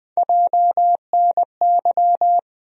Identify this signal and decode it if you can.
Morse code, heard as a keyed tone, about 20 wpm, 700 Hz. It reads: JNY
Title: USA Call Sign Postfixes - 3 Letters 20wpm